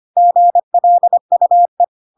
Morse code, heard as a keyed tone, about 25 wpm, 700 Hz. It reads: GLUE
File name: Pinocchio / Pinocchio ch8 25 25 ik1hge